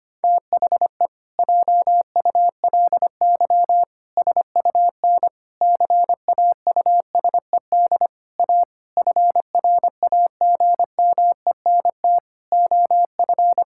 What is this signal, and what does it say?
Morse code, heard as a keyed tone, about 25 wpm, 700 Hz. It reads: THE JULY SUN CAUSED A FRAGMENT OF
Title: Pangram Challenge v2 - 25wpm